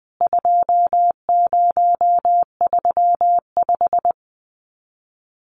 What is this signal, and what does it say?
Morse code, heard as a keyed tone, about 19 wpm, 700 Hz. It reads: 2035
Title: CWT Exchange 20wpm